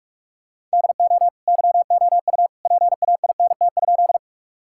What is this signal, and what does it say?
Morse code, heard as a keyed tone, about 41 wpm, 700 Hz. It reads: DO YOU PAINT?
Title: Sentences from Top 1000 Words - Review Part 6 - 45wpm